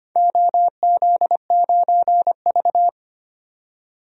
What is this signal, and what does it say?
Morse code, heard as a keyed tone, about 25 wpm, 700 Hz. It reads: OZ9V
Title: DXCC-100 Call Signs - 4 Characters - 25wpm